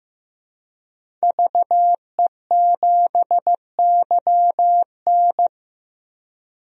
Morse code, heard as a keyed tone, about 16 wpm, 700 Hz. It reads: VE7YN
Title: DXCC-100 Call Signs - 5 Characters - 15wpm